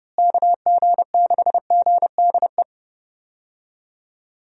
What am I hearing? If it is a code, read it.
Morse code, heard as a keyed tone, about 32 wpm, 700 Hz. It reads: KG6GDE
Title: USA Call Signs - 6 Characters - Repeated 2x - 30wpm